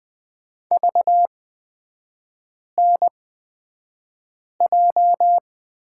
Morse code, heard as a keyed tone, about 20 wpm, spaced as for 5 wpm, 700 Hz. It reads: VNJ
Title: 250 Lesson 40e - BK - Three Characters 20wpm